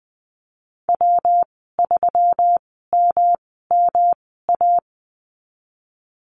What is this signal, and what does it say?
Morse code, heard as a keyed tone, about 20 wpm, spaced as for 15 wpm, 700 Hz. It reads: W3MMA
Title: 154 Lesson 25g - Letter p - Call Signs 20wpm